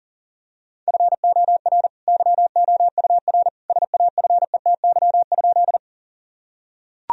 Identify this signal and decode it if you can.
Morse code, heard as a keyed tone, about 37 wpm, 700 Hz. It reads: FOR YOUR SAFETY?
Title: Sentences from Top 1000 Words - Review Part 4 - 40wpm